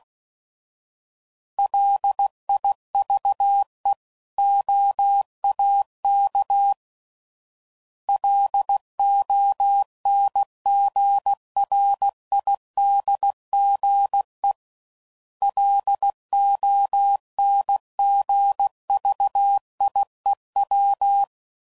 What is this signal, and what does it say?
Morse code, heard as a keyed tone, about 16 wpm, 800 Hz. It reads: LIVE OAK LONGRIDGE LONGVIEW
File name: CW Code Groups @15WPM / fromSM's / Cities